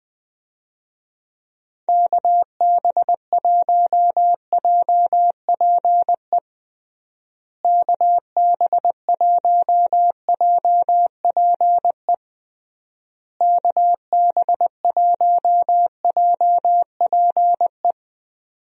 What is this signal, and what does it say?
Morse code, heard as a keyed tone, about 20 wpm, 700 Hz. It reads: KB1JPE KB1JPE KB1JPE
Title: USA Call Signs - 6 Characters - Repeated 3x - 20wpm